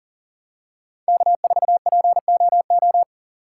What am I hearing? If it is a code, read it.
Morse code, heard as a keyed tone, about 40 wpm, 700 Hz. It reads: K4POO
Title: USA Call Signs - 5 Characters - 6x Character Spacing - 40wpm